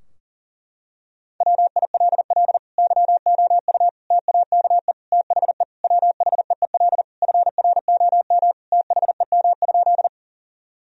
Morse code, heard as a keyed tone, about 40 wpm, 700 Hz. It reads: WILL YOU TAKE THE WHEEL FROM THEM?
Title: Sentences from Top 1000 Words - Review Part 3 - 40wpm